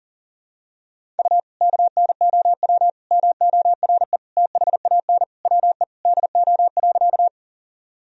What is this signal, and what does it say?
Morse code, heard as a keyed tone, about 38 wpm, 700 Hz. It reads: U KNOW MORE THAN WE DO.
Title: Sentences from Top 100 Words and Abbreviations 40wpm